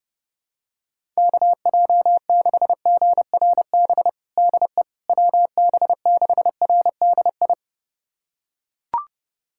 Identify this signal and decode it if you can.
Morse code, heard as a keyed tone, about 31 wpm, 700 Hz. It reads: KJ6GRB DE WB6RDI
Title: USA Callsign DE Callsign - 6 Characters - 30wpm